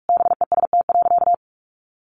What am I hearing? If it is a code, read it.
Morse code, heard as a keyed tone, about 44 wpm, 700 Hz. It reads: BEST.
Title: Sentences from Top 900 Words 45wpm